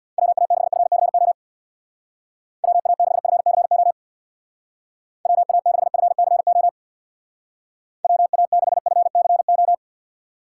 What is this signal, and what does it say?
Morse code, heard as a keyed tone, about 60 wpm, 700 Hz. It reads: WA6FCO WA6FCO WA6FCO WA6FCO
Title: Speed Racing - USA 2x3 Call Signs 50wpm